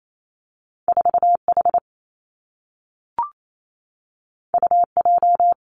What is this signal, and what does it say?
Morse code, heard as a keyed tone, about 28 wpm, 700 Hz. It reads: 4 H U J
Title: Two Challenging Letter-Number Combinations 28wpm